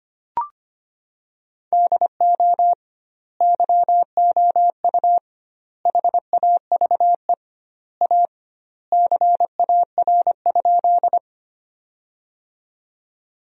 Morse code, heard as a keyed tone, about 25 wpm, 700 Hz. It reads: DO YOU HAVE A CAR?
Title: 2x Word-Spacing - Sentences from Top 300 Words - 25wpm